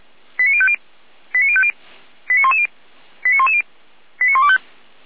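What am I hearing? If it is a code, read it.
Five-tone selective call (ZVEI), heard as quick runs of five tones, tones of about 70 ms. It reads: 89050 89050 89110 89110 89125
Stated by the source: Five-tone ZVEI selective call received by SDR.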